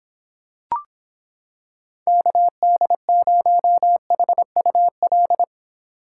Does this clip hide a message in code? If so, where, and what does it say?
Yes: Morse code from 2.1 to 5.4 s, heard as a keyed tone, about 26 wpm, 700 Hz. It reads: KD0HUL